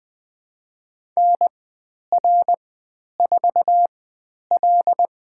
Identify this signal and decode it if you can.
Morse code, heard as a keyed tone, about 20 wpm, spaced as for 10 wpm, 700 Hz. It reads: NR4L
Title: 102 Lesson 17g - Letter u - Call Signs 20wpm